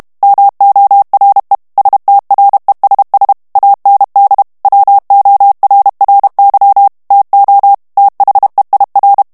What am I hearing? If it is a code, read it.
Morse code, heard as a keyed tone, about 32 wpm, 800 Hz. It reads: MORE STRESS AND WORRY TO THEIR
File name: morse mp3 / ANGLAIS texte clair 1 vitesse 30 mots minute